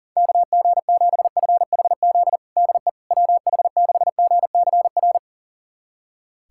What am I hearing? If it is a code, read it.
Morse code, heard as a keyed tone, about 37 wpm, 700 Hz. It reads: KG7FHZ DE WH6GCR
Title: USA Callsign DE Callsign - 6 Characters - 40wpm